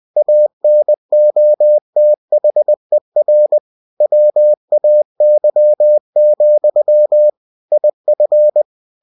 Morse code, heard as a keyed tone, about 20 wpm, 600 Hz. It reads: ANOTHER WAY, IF
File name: cw-books / winnie-the-pooh / winnie-20wpm-page-0000